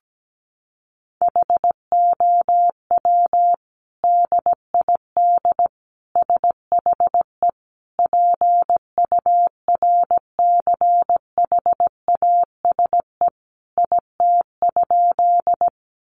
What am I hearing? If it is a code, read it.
Morse code, heard as a keyed tone, about 17 wpm, 700 Hz. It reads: HOW DID SHE PURCHASE IT?
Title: Sentences from Top 800 Words 17wpm